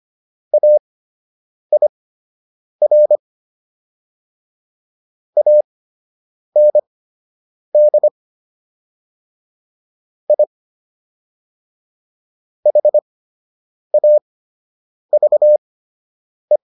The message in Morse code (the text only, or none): AIR AND I HAVE